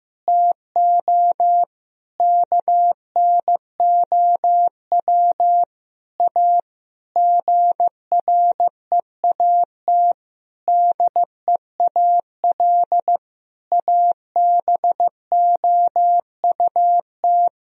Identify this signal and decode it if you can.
Morse code, heard as a keyed tone, about 15 wpm, 700 Hz. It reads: TO KNOW A GREAT DEAL ABOUT
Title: Sentences from Top 1000 Words - Review Part 1 - 15wpm